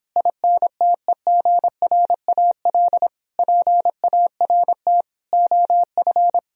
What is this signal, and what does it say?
Morse code, heard as a keyed tone, about 26 wpm, 700 Hz. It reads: INTEGRAL PART OF